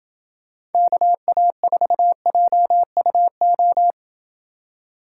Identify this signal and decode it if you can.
Morse code, heard as a keyed tone, about 29 wpm, 700 Hz. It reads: KA4JUO